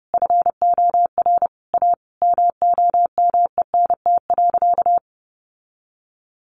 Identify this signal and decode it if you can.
Morse code, heard as a keyed tone, about 30 wpm, 700 Hz. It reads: FOR A MOMENT.
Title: Sentences from Top 1500 Words 30wpm